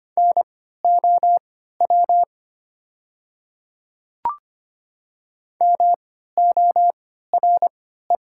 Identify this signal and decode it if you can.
Morse code, heard as a keyed tone, about 25 wpm, spaced as for 14 wpm, 700 Hz. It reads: NOW MORE
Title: 3x Character Spacing - Top 200 Words - 25wpm